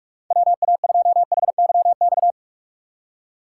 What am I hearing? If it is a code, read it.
Morse code, heard as a keyed tone, about 46 wpm, 700 Hz. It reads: WA2HYX
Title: USA Call Signs - 6 Characters - 4x Character Spacing - 45wpm